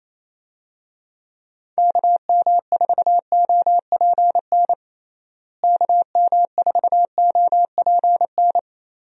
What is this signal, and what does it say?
Morse code, heard as a keyed tone, about 28 wpm, 700 Hz. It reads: KM4OPN KM4OPN
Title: USA Call Signs - 6 Characters - Repeated 2x - 28wpm